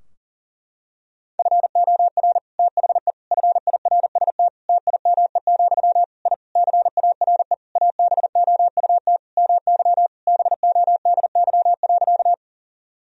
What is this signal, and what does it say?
Morse code, heard as a keyed tone, about 40 wpm, 700 Hz. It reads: FOR THE FIRST TIME, I CARE ABOUT MY BODY.
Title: Sentences from Top 1000 Words - Review Part 2 - 40wpm